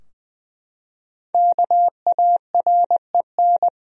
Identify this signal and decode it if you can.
Morse code, heard as a keyed tone, about 21 wpm, 700 Hz. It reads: KAREN
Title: USA Common Names - Max 5 Letters - 20wpm